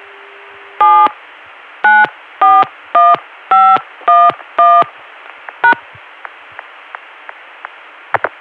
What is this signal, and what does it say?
DTMF, heard as telephone touch tones, tones of about 220 ms. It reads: 7941511 #